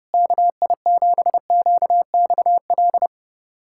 Morse code, heard as a keyed tone, about 30 wpm, 700 Hz. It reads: KI7QXL
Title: USA Callsign DE Callsign - 6 Characters - 30wpm